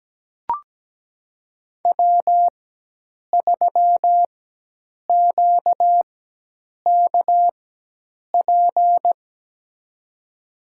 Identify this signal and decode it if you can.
Morse code, heard as a keyed tone, about 17 wpm, spaced as for 8 wpm, 700 Hz. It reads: W3QKP
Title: USA Call Signs - 5 Characters - 4x Character Spacing - 17wpm